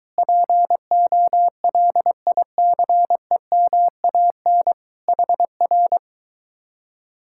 Morse code, heard as a keyed tone, about 23 wpm, 700 Hz. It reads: POLICEMAN HR